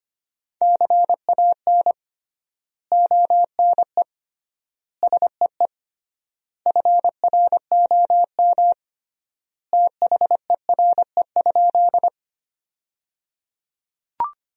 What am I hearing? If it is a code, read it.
Morse code, heard as a keyed tone, about 25 wpm, 700 Hz. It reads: CAN ONE SEE FROM THERE?
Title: 3x Word-Spacing - Sentences from Top 100 Words - 25wpm